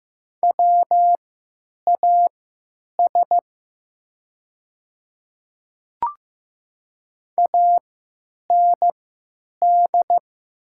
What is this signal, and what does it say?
Morse code, heard as a keyed tone, about 15 wpm, spaced as for 9 wpm, 700 Hz. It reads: WAS AND
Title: 3x Character Spacing - Top 100 Words - 15wpm